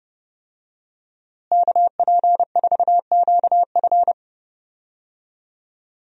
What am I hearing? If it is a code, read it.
Morse code, heard as a keyed tone, about 30 wpm, 700 Hz. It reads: KP4QF
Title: DXCC-100 Call Signs - 5 Characters - Repeated 2x - 30wpm